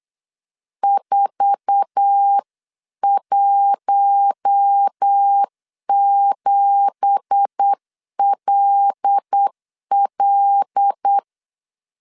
Morse code, heard as a keyed tone, about 8 wpm, 800 Hz. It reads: 417LL
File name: CW / DET / 1 9 1500C S A 8wpm 5637